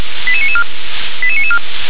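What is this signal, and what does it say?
Five-tone selective call (ZVEI), heard as quick runs of five tones, tones of about 70 ms. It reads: 88994 88994
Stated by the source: ZVEI five-tone selective call received by SDR.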